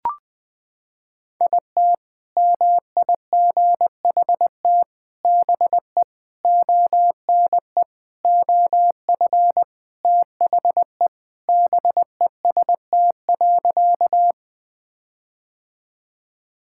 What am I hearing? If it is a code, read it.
Morse code, heard as a keyed tone, about 20 wpm, 700 Hz. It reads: IT MIGHT BE ONE OF THE BEST.
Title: Sentences with 4-Word Phrases - 20wpm